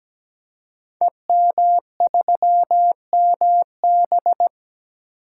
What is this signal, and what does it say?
Morse code, heard as a keyed tone, about 17 wpm, 700 Hz. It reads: EM3MB
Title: Code Groups - 5 Letter:Numbers - 17wpm